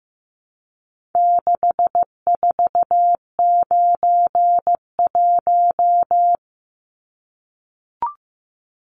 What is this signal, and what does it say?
Morse code, heard as a keyed tone, about 15 wpm, 700 Hz. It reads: 6491
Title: Numbers - 4-Digit 15wpm